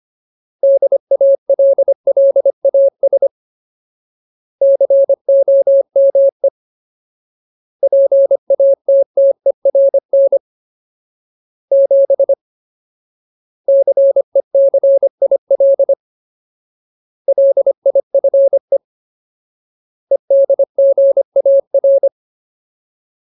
Morse code, heard as a keyed tone, about 25 wpm, 550 Hz. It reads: DALLAS COME PATTERN 7 CECIL LIFE EDGAR
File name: Ultimate Word List / 25WPM Ultimate